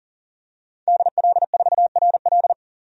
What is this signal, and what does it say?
Morse code, heard as a keyed tone, about 40 wpm, 700 Hz. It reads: DL4RL
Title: DXCC-100 Call Signs - 5 Characters - 40wpm